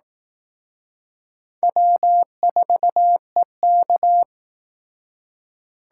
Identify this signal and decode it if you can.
Morse code, heard as a keyed tone, about 18 wpm, 700 Hz. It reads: W4EK
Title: Speed Racing - USA 1x2 Call Signs 15wpm